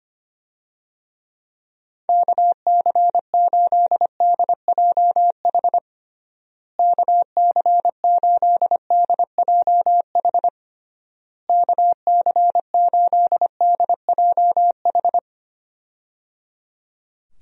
Morse code, heard as a keyed tone, about 25 wpm, 700 Hz. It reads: KC8DJH KC8DJH KC8DJH
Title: USA Call Signs - 6 Characters - Repeated 3x - 25wpm